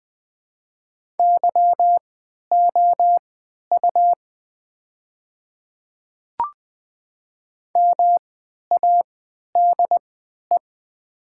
Morse code, heard as a keyed tone, about 20 wpm, spaced as for 11 wpm, 700 Hz. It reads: YOU MADE
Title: 3x Character Spacing - Top 200 Words - 20wpm